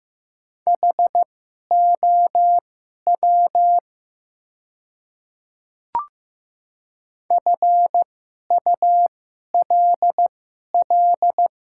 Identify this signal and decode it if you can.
Morse code, heard as a keyed tone, about 15 wpm, spaced as for 11 wpm, 700 Hz. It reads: HOW FULL
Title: 2x Character Spacing - Top 500 Words - 15wpm